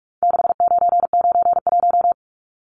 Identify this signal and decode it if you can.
Morse code, heard as a keyed tone, about 44 wpm, 700 Hz. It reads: 6991
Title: Numbers - 4-Digit 45wpm